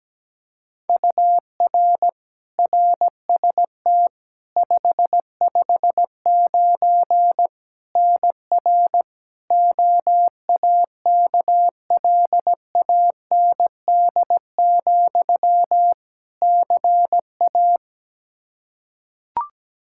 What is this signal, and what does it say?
Morse code, heard as a keyed tone, about 17 wpm, 700 Hz. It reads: UR RST 559 NR OAKLAND, CA
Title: K1USN SST QSO Practice 17wpm